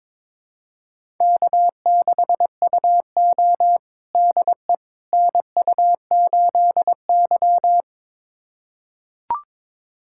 Morse code, heard as a keyed tone, about 22 wpm, 700 Hz. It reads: K6UO DE NU8Y
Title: USA Callsign DE Callsign - 4 Characters - 22wpm